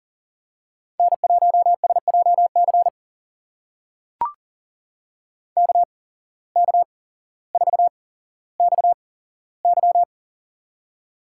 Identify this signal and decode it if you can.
Morse code, heard as a keyed tone, about 40 wpm, spaced as for 17 wpm, 700 Hz. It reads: N1SJC KK4XY
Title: USA Call Signs - 5 Characters - 8x Character Spacing - 40wpm